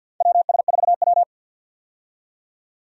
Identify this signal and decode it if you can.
Morse code, heard as a keyed tone, about 53 wpm, 700 Hz. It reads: WS4W
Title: USA Call Signs - 4 Characters - 50wpm